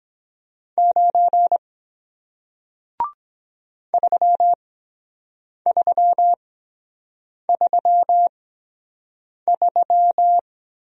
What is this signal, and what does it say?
Morse code, heard as a keyed tone, about 23 wpm, spaced as for 7 wpm, 700 Hz. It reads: 9 3333